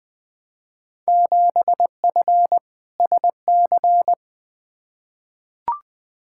Morse code, heard as a keyed tone, about 21 wpm, 700 Hz. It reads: 7F SC
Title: Field Day Exchanges Encore - 20wpm